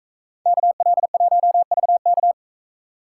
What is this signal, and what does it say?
Morse code, heard as a keyed tone, about 42 wpm, 700 Hz. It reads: KL1VK